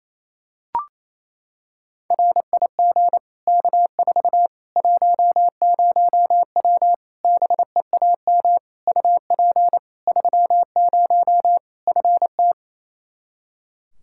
Morse code, heard as a keyed tone, about 28 wpm, 700 Hz. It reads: RIG K4 10W BEAM UP 30 FT
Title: K1USN SST QSO Practice 28wpm